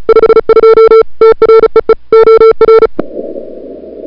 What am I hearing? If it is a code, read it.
Morse code, heard as a keyed tone, about 36 wpm, 440 Hz. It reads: 52 TREE OR
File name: CWSprint / February 2007 CW Sprint / 01-2007 CW Sprint Hour1